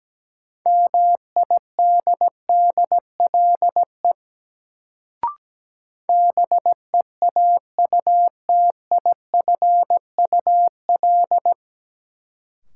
Morse code, heard as a keyed tone, about 17 wpm, 700 Hz. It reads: MIDDLE BEAUTIFUL
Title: Sentences from Top 900 Words - New Word Drill - 17wpm